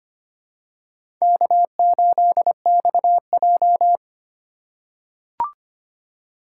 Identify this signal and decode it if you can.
Morse code, heard as a keyed tone, about 25 wpm, 700 Hz. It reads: K8XJ
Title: K1USN SST QSO Practice - 25wpm